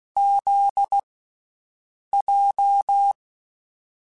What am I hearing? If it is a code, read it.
Morse code, heard as a keyed tone, about 16 wpm, 800 Hz. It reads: Z J
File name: Basic Sets / Basic Letter Groups Collective7 / JZ17@10